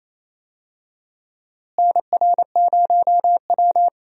Morse code, heard as a keyed tone, about 28 wpm, 700 Hz. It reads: NR0W
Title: USA Call Signs - 4 Characters - Repeated 2x - 28wpm